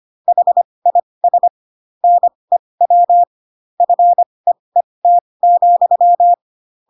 Morse code, heard as a keyed tone, about 25 wpm, spaced as for 20 wpm, 700 Hz. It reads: HIS NEW FEET,
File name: Pinocchio / Pinocchio ch8 15 25 ik1hge